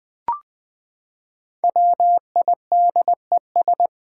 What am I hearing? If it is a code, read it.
Morse code, heard as a keyed tone, about 20 wpm, 700 Hz. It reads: WIDES
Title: Build A Word - 5 Letters At A Time - 10-Letter Words - 20wpm